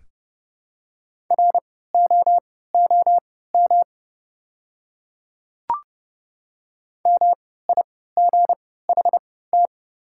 Morse code, heard as a keyed tone, about 30 wpm, spaced as for 17 wpm, 690 Hz. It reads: ROOM MIGHT
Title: 3x Character Spacing - Top 300 Words - 30wpm